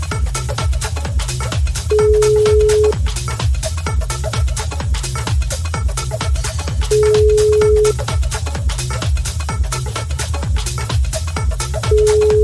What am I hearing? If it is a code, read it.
Ringback tone (Europe, 425 Hz).